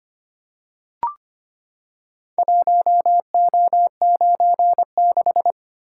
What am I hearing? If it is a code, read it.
Morse code, heard as a keyed tone, about 25 wpm, 700 Hz. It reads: 1O96
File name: practice / 25wpm / Four Letter-Number Combinations - 25wpm